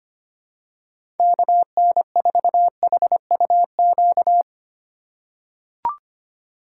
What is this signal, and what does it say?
Morse code, heard as a keyed tone, about 25 wpm, 690 Hz. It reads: KN4HUQ
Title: USA Call Signs - 6 Characters - 2x Character Spacing - 25wpm